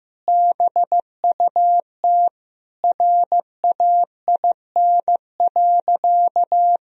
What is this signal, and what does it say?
Morse code, heard as a keyed tone, about 15 wpm, 700 Hz. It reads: BUT RAIN.
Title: Sentences from Top 1000 Words - Review Part 1 - 15wpm